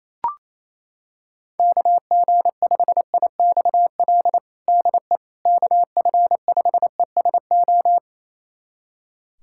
Morse code, heard as a keyed tone, about 28 wpm, 700 Hz. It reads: KG5IXL DE KF5ESO
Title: USA Callsign DE Callsign - 6 Characters - 28wpm